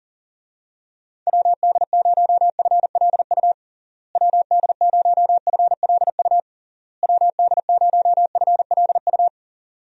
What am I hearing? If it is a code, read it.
Morse code, heard as a keyed tone, about 40 wpm, 700 Hz. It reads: WD0FLU WD0FLU WD0FLU
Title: USA Call Signs - 6 Characters - Repeated 3x - 40wpm